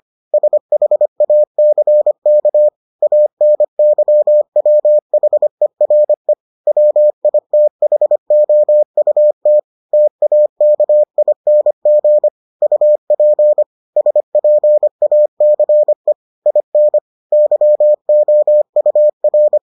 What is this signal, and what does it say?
Morse code, heard as a keyed tone, about 25 wpm, 600 Hz. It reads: SHACK ANYWHERE WITHOUT TAKING UP SPACE IN YOUR